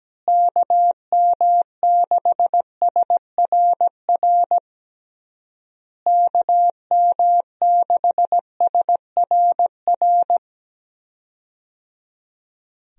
Morse code, heard as a keyed tone, about 17 wpm, 700 Hz. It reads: KM6SRR KM6SRR
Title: USA Call Signs - 6 Characters - Repeated 3x - 17wpm